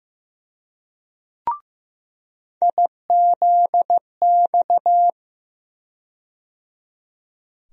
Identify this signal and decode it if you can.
Morse code, heard as a keyed tone, about 16 wpm, 700 Hz. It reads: IZX
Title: USA Call Sign Postfixes - 3 Letters 15wpm